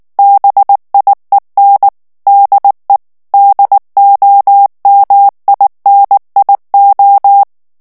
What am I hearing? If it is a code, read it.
Morse code, heard as a keyed tone, about 19 wpm, 800 Hz. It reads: BIEN DE DOMINIO
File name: ESPAGNOL TEXTE EN CLAIR 2 VITESSE 18 MOTS MINUTES